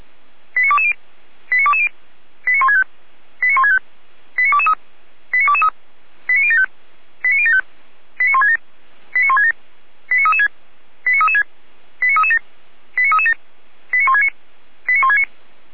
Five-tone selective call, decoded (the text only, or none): 89220 89220 89165 89165 89223 89223 89075 89075 89167 89167 89336 89336 89337 89337 89169 89169